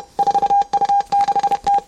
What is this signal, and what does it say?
Morse code, heard as a keyed tone, about 31 wpm, 800 Hz. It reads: 4U6T